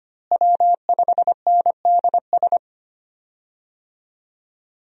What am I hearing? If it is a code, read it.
Morse code, heard as a keyed tone, about 25 wpm, 700 Hz. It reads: W5NDS
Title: USA Call Signs - 5 Characters - Repeated 2x - 25wpm